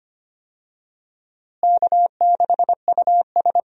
Morse code, heard as a keyed tone, about 25 wpm, 700 Hz. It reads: K6US
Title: USA Call Signs - 4 Characters - Repeated 2x - 25wpm